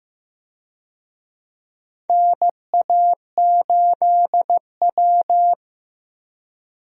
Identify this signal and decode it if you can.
Morse code, heard as a keyed tone, about 15 wpm, 700 Hz. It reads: NA8W
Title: USA Call Signs - 4 Characters - Repeated 3x - 15wpm